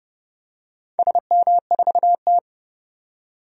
Morse code, heard as a keyed tone, about 29 wpm, 700 Hz. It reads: SM4T
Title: DXCC-100 Call Signs - 4 Characters - 30wpm